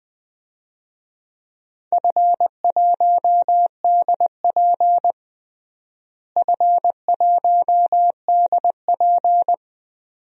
Morse code, heard as a keyed tone, about 20 wpm, 700 Hz. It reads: F1DP F1DP
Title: DXCC-100 Call Signs - 4 Characters - Repeated 2x - 20wpm